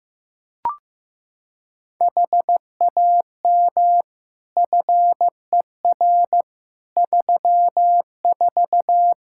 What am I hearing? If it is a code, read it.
Morse code, heard as a keyed tone, about 15 wpm, 700 Hz. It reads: HAM FER 34
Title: Practice QSO - Exchange 3 - More Info - 15wpm